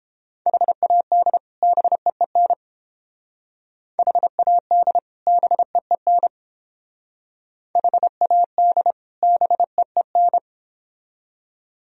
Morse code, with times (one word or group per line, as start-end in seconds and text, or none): 0.5-1.4 HAD
1.6-2.5 BEEN
4.0-5.0 HAD
5.3-6.3 BEEN
7.8-8.9 HAD
9.2-10.4 BEEN